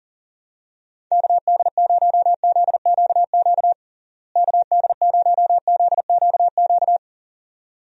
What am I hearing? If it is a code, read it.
Morse code, heard as a keyed tone, about 40 wpm, 700 Hz. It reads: KD0ZQQ KD0ZQQ
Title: USA Call Signs - 6 Characters - Repeated 2x - 40wpm